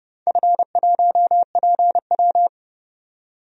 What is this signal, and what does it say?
Morse code, heard as a keyed tone, about 30 wpm, 700 Hz. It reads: F1PW
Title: DXCC-100 Call Signs - 4 Characters - 30wpm